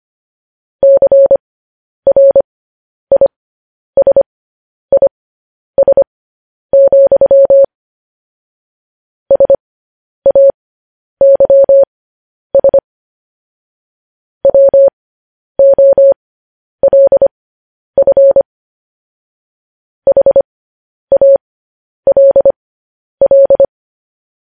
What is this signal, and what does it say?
Morse code, heard as a keyed tone, about 25 wpm, spaced as for 10 wpm, 550 Hz. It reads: CRISIS, SAYS WOLF HALL
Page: news-headlines-in-morse-code-at-10-wpm